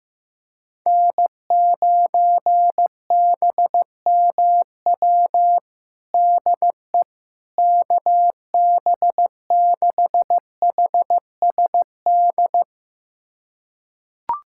Morse code, heard as a keyed tone, about 15 wpm, 700 Hz. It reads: N9BMW DE KB6HSD
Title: USA Callsign DE Callsign 15wpm